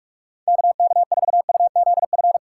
Morse code, heard as a keyed tone, about 57 wpm, 700 Hz. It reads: KK4UZF